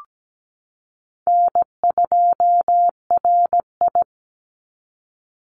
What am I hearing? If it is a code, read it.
Morse code, heard as a keyed tone, about 17 wpm, 700 Hz. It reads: N2RI
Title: Call Signs Repeated 3x - 17wpm